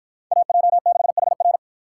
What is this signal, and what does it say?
Morse code, heard as a keyed tone, about 53 wpm, 700 Hz. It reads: AJ6HR